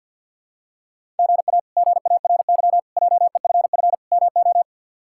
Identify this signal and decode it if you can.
Morse code, heard as a keyed tone, about 46 wpm, 700 Hz. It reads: GA GARY JEFF MO
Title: K1USN SST QSO Practice - 50wpm